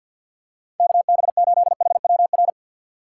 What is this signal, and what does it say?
Morse code, heard as a keyed tone, about 50 wpm, 700 Hz. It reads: KB8HWR
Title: USA Call Signs - 6 Characters - 6x Character Spacing - 50wpm